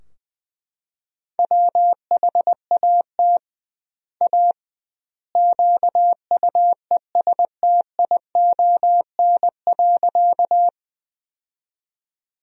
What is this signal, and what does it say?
Morse code, heard as a keyed tone, about 21 wpm, 700 Hz. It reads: WHAT A QUESTION.
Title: 2x Word-Spacing - Sentences from Top 400 Words - 20wpm